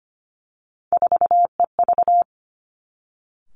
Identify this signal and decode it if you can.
Morse code, heard as a keyed tone, about 25 wpm, 700 Hz. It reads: 4EV
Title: Three Challenging Letter-Number Combinations 25wpm